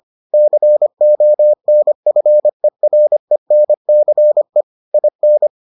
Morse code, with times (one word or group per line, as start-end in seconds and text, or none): 0.3-4.6 CONFERENCE
4.9-5.5 IN